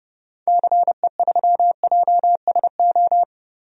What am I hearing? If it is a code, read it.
Morse code, heard as a keyed tone, about 30 wpm, 700 Hz. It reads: CE3JSO
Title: DXCC-100 Call Signs - 6 Characters - Repeated 2x - 30wpm